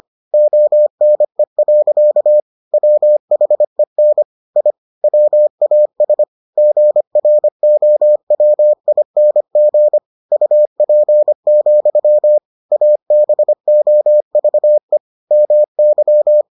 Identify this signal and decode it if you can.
Morse code, heard as a keyed tone, about 25 wpm, 600 Hz. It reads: ONE. WHEN I WAS GROWING UP, ABOVE MY